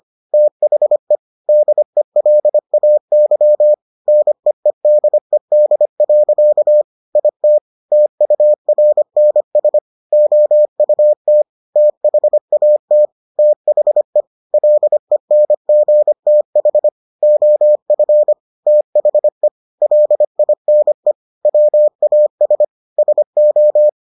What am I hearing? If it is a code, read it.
Morse code, heard as a keyed tone, about 25 wpm, 600 Hz. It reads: THE DELAY NEEDED. IT TURNS OUT THAT THE LENGTH OF THE LINE WAS SO